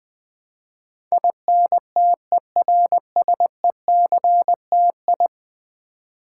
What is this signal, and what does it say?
Morse code, heard as a keyed tone, about 20 wpm, 700 Hz. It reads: INTERSECTI
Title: Build A Word - 5 Letters At A Time - 12-Letter Words - 20wpm